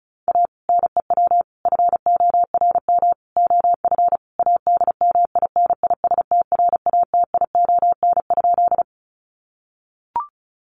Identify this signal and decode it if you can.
Morse code, heard as a keyed tone, about 34 wpm, 700 Hz. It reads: A NEW FORM OF ADMINISTRATION?
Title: Sentences from Top 1300 Words 35wpm